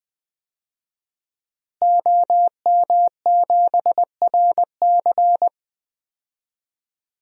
Morse code, heard as a keyed tone, about 20 wpm, 700 Hz. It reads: OM7RC
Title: DXCC-100 Call Signs - 5 Characters - Repeated 2x - 20wpm